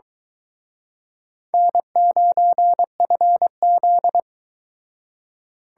Morse code, heard as a keyed tone, about 23 wpm, 700 Hz. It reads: N9FZ